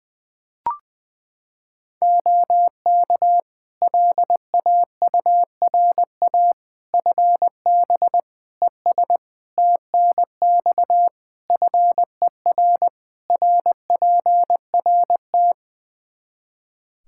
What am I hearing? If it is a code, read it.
Morse code, heard as a keyed tone, about 20 wpm, 700 Hz. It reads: OK LAURA FB ES TNX FER RPRT
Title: K1USN SST QSO Practice 20wpm